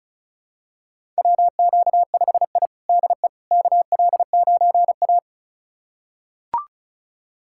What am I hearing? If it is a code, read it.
Morse code, heard as a keyed tone, about 34 wpm, 700 Hz. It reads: WQ5I DE KL9A
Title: USA Callsign DE Callsign - 4 Characters - 35wpm